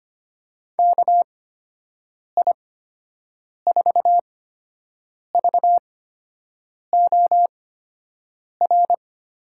Morse code, heard as a keyed tone, about 25 wpm, spaced as for 7 wpm, 700 Hz. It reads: KI4VOR
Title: USA Call Signs - 6 Characters - 8x Character Spacing - 25wpm